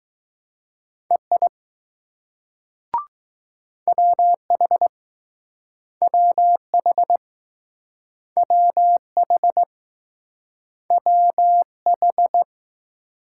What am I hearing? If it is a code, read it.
Morse code, heard as a keyed tone, about 20 wpm, 700 Hz. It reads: EI WH WH WH WH